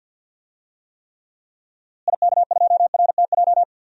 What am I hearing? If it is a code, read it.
Morse code, heard as a keyed tone, about 51 wpm, 700 Hz. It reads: IK2RTJ
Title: DXCC-100 Call Signs - 6 Characters - Repeated 2x - 50wpm